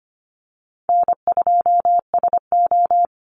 Morse code, heard as a keyed tone, about 25 wpm, 700 Hz. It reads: N2SO
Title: Call Signs - 25wpm